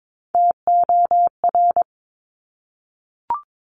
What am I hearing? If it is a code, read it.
Morse code, heard as a keyed tone, about 22 wpm, 700 Hz. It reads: TOR
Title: Top Three Letter Combinations in Words 22wpm